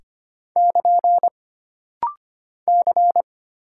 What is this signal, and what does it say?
Morse code, heard as a keyed tone, about 25 wpm, 700 Hz. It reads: <KN> C
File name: Single Character (Letter:Number:Punctuation:Prosign) - Rapid-Fire - 25wpm